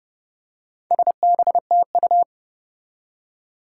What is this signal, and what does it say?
Morse code, heard as a keyed tone, about 30 wpm, 700 Hz. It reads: SBTU
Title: Four Challenging Letter-Number Combinations 30wpm